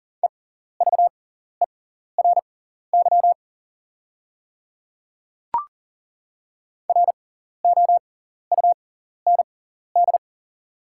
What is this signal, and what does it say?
Morse code, heard as a keyed tone, about 40 wpm, spaced as for 14 wpm, 690 Hz. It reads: EVERY ROUND
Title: 6x Character Spacing - Top 300 Words - 40wpm